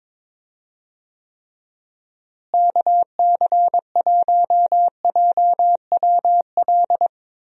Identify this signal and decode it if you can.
Morse code, heard as a keyed tone, about 22 wpm, 700 Hz. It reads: KC1JWL
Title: USA Call Signs - 6 Characters - Repeated 2x - 22wpm